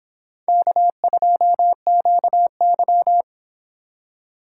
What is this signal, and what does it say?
Morse code, heard as a keyed tone, about 23 wpm, 700 Hz. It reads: K2QY